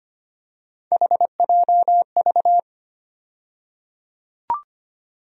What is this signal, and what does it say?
Morse code, heard as a keyed tone, about 25 wpm, 700 Hz. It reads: HJV
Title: USA Call Sign Postfixes - 3 Letters 25wpm